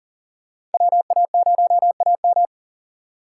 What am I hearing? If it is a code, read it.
Morse code, heard as a keyed tone, about 40 wpm, 700 Hz. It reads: WA0AM
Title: USA Call Signs - 5 Characters - 4x Character Spacing - 40wpm